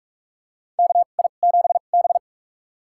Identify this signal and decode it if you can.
Morse code, heard as a keyed tone, about 46 wpm, spaced as for 32 wpm, 700 Hz. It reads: KI7B
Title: USA Call Signs - 4 Characters - 2x Character Spacing - 45wpm